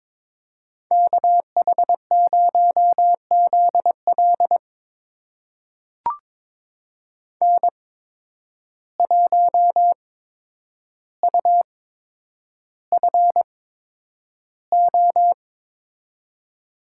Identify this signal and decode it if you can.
Morse code, heard as a keyed tone, about 22 wpm, spaced as for 9 wpm, 700 Hz. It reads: KH0ZL N1UFO
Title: USA Call Signs - 8x Character Spacing - 22wpm